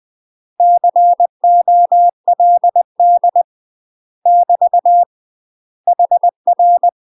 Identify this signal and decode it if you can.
Morse code, heard as a keyed tone, about 20 wpm, 700 Hz. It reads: COLD <BT> HR
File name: UK MASTER QSOs TRAINING / 20WPM / UK MASTER QSOs 0068